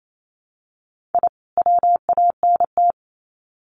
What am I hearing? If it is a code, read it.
Morse code, heard as a keyed tone, about 28 wpm, 700 Hz. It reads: I WANT
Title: Sets of 2 Words 28wpm